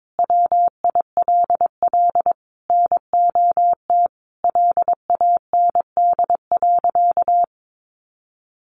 Morse code, heard as a keyed tone, about 22 wpm, 700 Hz. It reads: WILL NOT LAND.
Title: Sentences from Top 1500 Words 22wpm